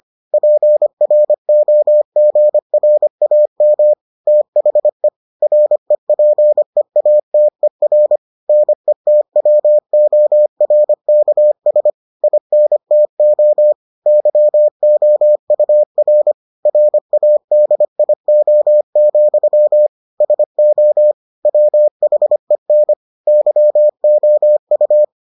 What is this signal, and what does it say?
Morse code, heard as a keyed tone, about 25 wpm, 600 Hz. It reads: PROGRAM THE REPEATER NETWORKS INTO YOUR RADIO, SO WHEN YOU